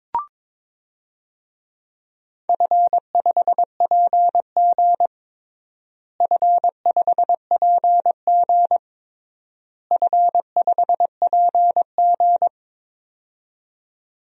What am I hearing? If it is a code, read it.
Morse code, heard as a keyed tone, about 22 wpm, 700 Hz. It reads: F5PG F5PG F5PG
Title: DXCC-100 Call Signs - 4 Characters - Repeated 3x - 22wpm